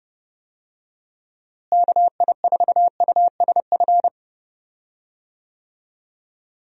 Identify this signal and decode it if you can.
Morse code, heard as a keyed tone, about 30 wpm, 700 Hz. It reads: KI4USF
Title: USA Call Signs - 6 Characters - Repeated 2x - 30wpm